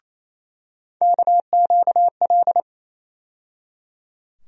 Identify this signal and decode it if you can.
Morse code, heard as a keyed tone, about 28 wpm, 700 Hz. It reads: KQL
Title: USA Call Sign Postfixes - 3 Letters 28wpm